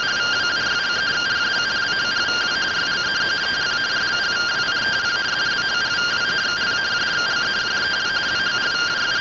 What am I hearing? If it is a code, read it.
Two-tone teleprinter signal, about 170 Hz shift.